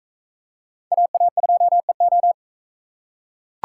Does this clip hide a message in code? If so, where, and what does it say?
Yes: Morse code from 0.9 to 2.3 s, heard as a keyed tone, about 42 wpm, 700 Hz. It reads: AA2EO